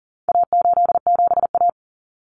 Morse code, heard as a keyed tone, about 40 wpm, 700 Hz. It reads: A87A
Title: Common Cut Numbers - 4-Digit 40wpm